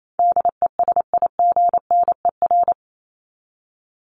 Morse code, heard as a keyed tone, about 28 wpm, 700 Hz. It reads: DESIGNER
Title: Top 1501-1750 Words 28wpm